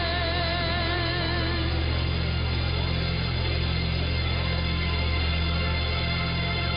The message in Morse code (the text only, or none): none